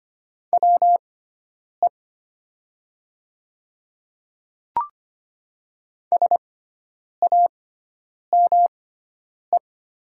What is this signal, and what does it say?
Morse code, heard as a keyed tone, about 25 wpm, spaced as for 9 wpm, 700 Hz. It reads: WE SAME
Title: 6x Character Spacing - Top 200 Words - 25wpm